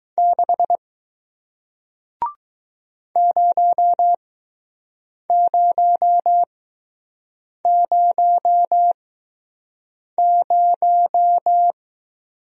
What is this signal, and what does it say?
Morse code, heard as a keyed tone, about 20 wpm, spaced as for 6 wpm, 700 Hz. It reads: 6 0000